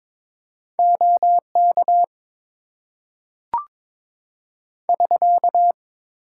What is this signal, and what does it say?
Morse code, heard as a keyed tone, about 22 wpm, 700 Hz. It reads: OK <SK>
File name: Top 90 QSO Elements - 8x Character Spacing - 22wpm